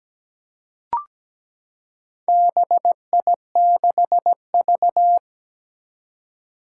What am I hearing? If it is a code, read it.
Morse code, heard as a keyed tone, about 17 wpm, 700 Hz. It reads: BI6V
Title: Four Challenging Letter-Number Combinations 17wpm